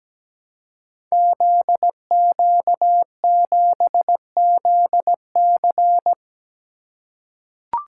Morse code, heard as a keyed tone, about 17 wpm, 700 Hz. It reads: ZQ7ZC
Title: Code Groups - 5 Letter:Numbers - 17wpm